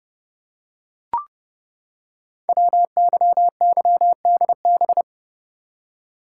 Morse code, heard as a keyed tone, about 30 wpm, 700 Hz. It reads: WYYDB
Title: Code Groups - 5 Letters - 30wpm